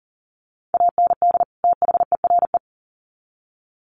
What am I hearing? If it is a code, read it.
Morse code, heard as a keyed tone, about 40 wpm, 700 Hz. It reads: AND THERE
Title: Sets of 2 Words 40wpm